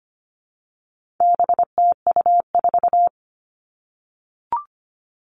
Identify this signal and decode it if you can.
Morse code, heard as a keyed tone, about 25 wpm, 700 Hz. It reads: BTU4
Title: All Cut Numbers - 4-Digit 25wpm